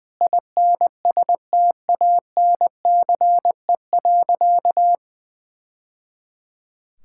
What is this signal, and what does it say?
Morse code, heard as a keyed tone, about 20 wpm, 700 Hz. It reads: INSTANCE.
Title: Sentences from Top 1000 Words - Review Part 5 - 20wpm